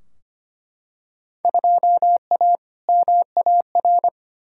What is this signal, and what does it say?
Morse code, heard as a keyed tone, about 25 wpm, 700 Hz. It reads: 2A MAR
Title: Field Day Exchanges - 25wpm